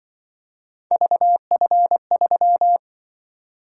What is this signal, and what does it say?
Morse code, heard as a keyed tone, about 24 wpm, 700 Hz. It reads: VF3
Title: Speed-Racing - Three Letter-Number Combinations - 20wpm